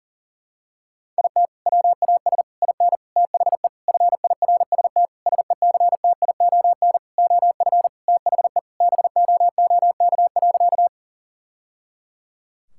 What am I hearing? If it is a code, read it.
Morse code, heard as a keyed tone, about 40 wpm, 700 Hz. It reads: IT WAS IN THE FIRST SECTION OF THE BOOK.
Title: Sentences from Top 1000 Words - Review Part 5 - 40wpm